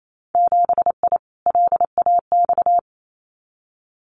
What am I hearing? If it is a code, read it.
Morse code, heard as a keyed tone, about 28 wpm, 700 Hz. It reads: 7I LAX
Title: Winter Field Day Exchanges Encore 28wpm